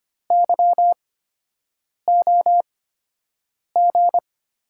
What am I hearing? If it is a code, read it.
Morse code, heard as a keyed tone, about 25 wpm, spaced as for 7 wpm, 700 Hz. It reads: YOG
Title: USA Call Signs - 6 Characters - 8x Character Spacing - 25wpm